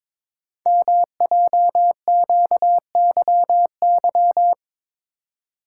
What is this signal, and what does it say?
Morse code, heard as a keyed tone, about 23 wpm, 700 Hz. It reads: MJQYY
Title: Code Groups - 5 Letters - 22wpm